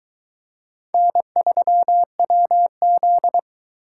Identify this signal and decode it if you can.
Morse code, heard as a keyed tone, about 22 wpm, 700 Hz. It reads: N3WZ